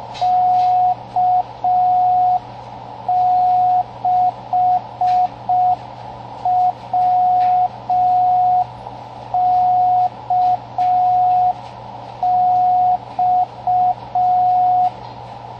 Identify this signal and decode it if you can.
Morse code, heard as a keyed tone, about 5 wpm, 730 Hz, with radio noise. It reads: K6WKX
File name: World's Slowest Beacon